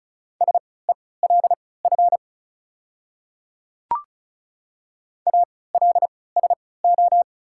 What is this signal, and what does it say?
Morse code, heard as a keyed tone, about 36 wpm, spaced as for 20 wpm, 700 Hz. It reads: SELF ALSO
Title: 3x Character Spacing - Top 200 Words - 35wpm